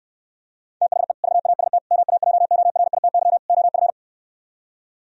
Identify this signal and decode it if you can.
Morse code, heard as a keyed tone, about 75 wpm, 700 Hz. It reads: THE VAST MAJORITY OF